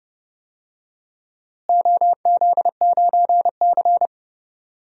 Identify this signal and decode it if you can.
Morse code, heard as a keyed tone, about 30 wpm, 700 Hz. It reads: OZ9C
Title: DXCC-100 Call Signs - 4 Characters - Repeated 2x - 30wpm